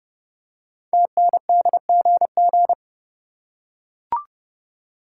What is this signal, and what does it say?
Morse code, heard as a keyed tone, about 30 wpm, 700 Hz. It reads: TNDGG
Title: Code Groups - 5 Letters - 30wpm